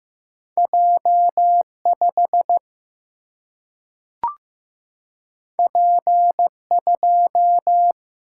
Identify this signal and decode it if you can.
Morse code, heard as a keyed tone, about 15 wpm, 700 Hz. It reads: J 5 P 2
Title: Single-Letter Single-Number 15wpm